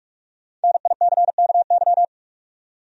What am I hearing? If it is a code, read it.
Morse code, heard as a keyed tone, about 46 wpm, 700 Hz. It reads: NICKY
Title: USA Common Names - Max 5 Letters - 45wpm